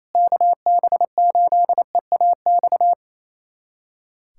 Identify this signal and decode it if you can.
Morse code, heard as a keyed tone, about 28 wpm, 700 Hz. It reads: KB8EAX
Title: USA Callsign DE Callsign - 6 Characters - 28wpm